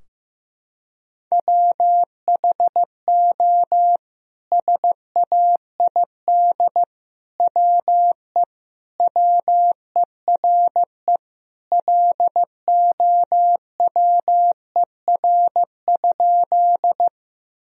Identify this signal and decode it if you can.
Morse code, heard as a keyed tone, about 15 wpm, 700 Hz. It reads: WHO SAID WE WERE LOWER?
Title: Sentences from Top 1000 Words - Review Part 4 - 15wpm